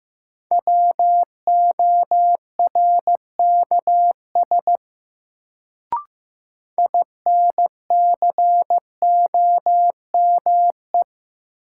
Morse code, heard as a keyed tone, about 16 wpm, 700 Hz. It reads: WORKS INCOME
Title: Sentences from Top 1000 Words - Review Part 5 - New Word Drill - 15wpm